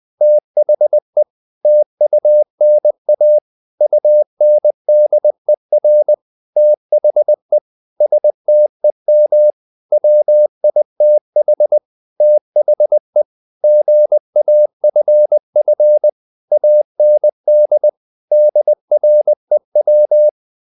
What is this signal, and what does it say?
Morse code, heard as a keyed tone, about 20 wpm, 600 Hz. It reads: THE TUNA UNDER THE STEM WITH THE GAFF AND DREW
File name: the old man and the sea / the old man and the sea part 1 20wpm